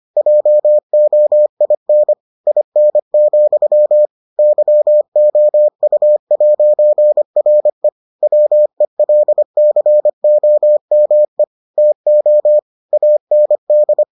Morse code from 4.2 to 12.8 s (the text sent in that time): YOU'RE WELCOME TO